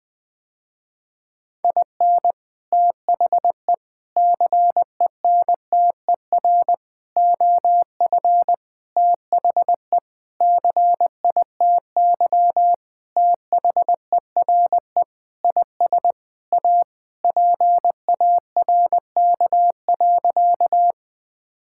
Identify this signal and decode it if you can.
Morse code, heard as a keyed tone, about 21 wpm, 700 Hz. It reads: IN THE CENTER OF THE CITY THERE IS A PARK.
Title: Sentences with 5-Word Phrases - Spoken Beginning - 20wpm